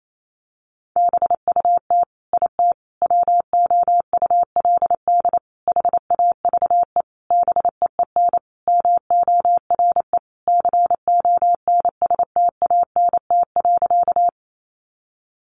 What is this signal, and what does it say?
Morse code, heard as a keyed tone, about 28 wpm, 700 Hz. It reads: BUT IT WOULD HAVE BEEN MORE CONSTANT.
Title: Sentences from Top 1500 Words 28wpm